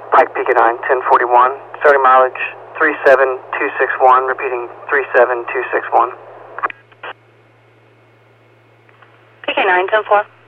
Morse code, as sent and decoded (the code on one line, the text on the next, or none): none